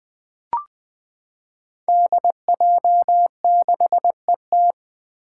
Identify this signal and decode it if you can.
Morse code, heard as a keyed tone, about 21 wpm, 700 Hz. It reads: DJ6ET
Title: Five Challenging Letter-Number Combinations 20wpm